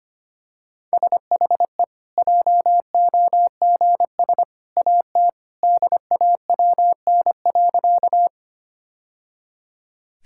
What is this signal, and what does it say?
Morse code, heard as a keyed tone, about 25 wpm, 700 Hz. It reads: SHE JOGS AT DAWN.
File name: Easy Sentences with Hints - 25wpm